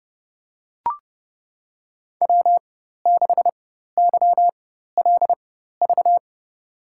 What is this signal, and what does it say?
Morse code, heard as a keyed tone, about 30 wpm, spaced as for 14 wpm, 700 Hz. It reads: W6YLV
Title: USA Call Signs - 5 Characters - 4x Character Spacing - 30wpm